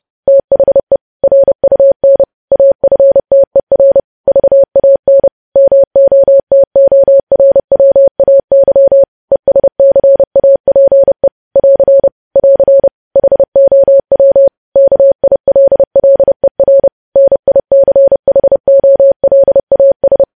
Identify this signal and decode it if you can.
Morse code, heard as a keyed tone, about 30 wpm, 550 Hz. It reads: THE RUN AFTER VAN MOTORWAY ESCAPE <AR> <AR> HOW KILLER NICHOLAS
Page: news headlines in morse code at 30 wpm